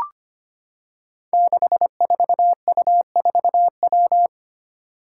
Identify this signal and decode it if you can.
Morse code, heard as a keyed tone, about 25 wpm, 700 Hz. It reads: 64U4W
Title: Five Challenging Letter-Number Combinations 25wpm